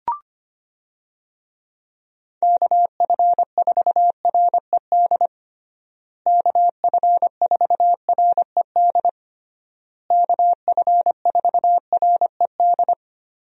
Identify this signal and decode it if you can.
Morse code, heard as a keyed tone, about 25 wpm, 700 Hz. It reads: KF4RED KF4RED KF4RED
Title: USA Call Signs - 6 Characters - Repeated 3x - 25wpm